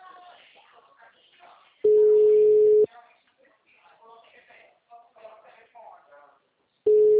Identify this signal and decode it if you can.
Ringback tone (Europe, 425 Hz), with speech.